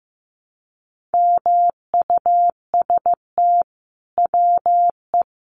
Morse code, heard as a keyed tone, about 15 wpm, 700 Hz. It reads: MUST WE
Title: Sentences from Top 1400 Words 15wpm